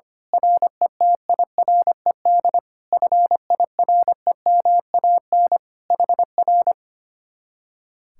Morse code, heard as a keyed tone, about 24 wpm, 700 Hz. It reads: RETIRED FIREMAN HR
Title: Speed-Racing Common QSO Phrases - 25wpm